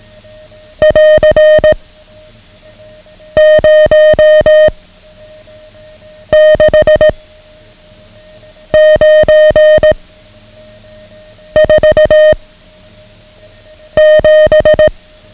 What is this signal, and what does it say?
Morse code, heard as a keyed tone, about 18 wpm, 620 Hz. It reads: <AR> 0 6 9 4 7
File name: ham radio / CodeQuick / Code Quick 09